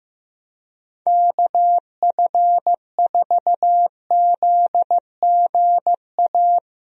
Morse code, heard as a keyed tone, about 15 wpm, 700 Hz. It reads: KF4ZGA
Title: USA Call Signs - 6 Characters - Repeated 3x - 15wpm